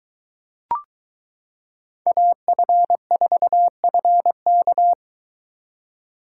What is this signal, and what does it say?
Morse code, heard as a keyed tone, about 22 wpm, 700 Hz. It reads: AF4FK